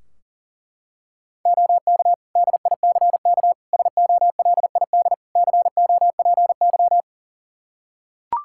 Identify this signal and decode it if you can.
Morse code, heard as a keyed tone, about 40 wpm, 700 Hz. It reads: OK DICK SOLID COPY
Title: Practice QSO - Exchange 3 - More Info - 40wpm